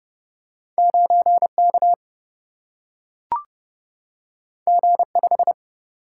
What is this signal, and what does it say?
Morse code, heard as a keyed tone, about 30 wpm, 700 Hz. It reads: 9K G5
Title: Single-Letter Single-Number 30wpm